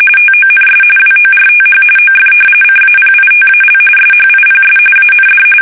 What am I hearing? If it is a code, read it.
RTTY, about 45.45 baud. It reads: TONS WORTH OF CAPITAL SHIPS AND
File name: fsk850 / battleshipnj